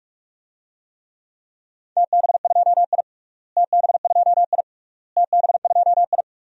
Morse code, heard as a keyed tone, about 42 wpm, 690 Hz. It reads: TB2I TB2I TB2I
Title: DXCC-100 Call Signs - 4 Characters - Repeated 3x - 45wpm